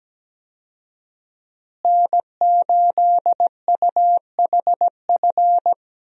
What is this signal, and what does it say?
Morse code, heard as a keyed tone, about 17 wpm, 700 Hz. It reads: N8UHF
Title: USA Call Signs - 5 Characters - Repeated 2x - 17wpm